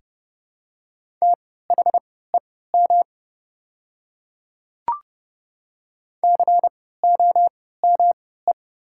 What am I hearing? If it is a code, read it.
Morse code, heard as a keyed tone, about 30 wpm, spaced as for 17 wpm, 700 Hz. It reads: THEM COME
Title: 3x Character Spacing - Top 200 Words - 30wpm